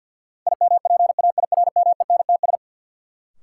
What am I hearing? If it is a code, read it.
Morse code, heard as a keyed tone, about 46 wpm, 700 Hz. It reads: IMPAIRMENTS
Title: Sets of 5 Words Encore - New Word Drill - 50wpm